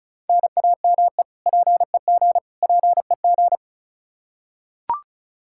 Morse code, heard as a keyed tone, about 34 wpm, 700 Hz. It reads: NAME PEG PEG
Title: Practice QSO - Exchange 1 - Beginning - 35wpm